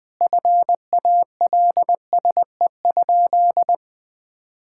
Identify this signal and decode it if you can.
Morse code, heard as a keyed tone, about 20 wpm, 700 Hz. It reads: FALSE?
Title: Sentences from Top 1600 Words 20wpm